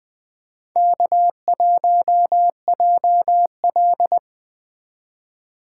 Morse code, heard as a keyed tone, about 21 wpm, 700 Hz. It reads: K1JL
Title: Speed Racing - USA 1x2 Call Signs 17wpm